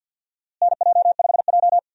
Morse code, heard as a keyed tone, about 52 wpm, 700 Hz. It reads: NJ5J
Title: USA Call Signs - 4 Characters - Repeated 3x - 50wpm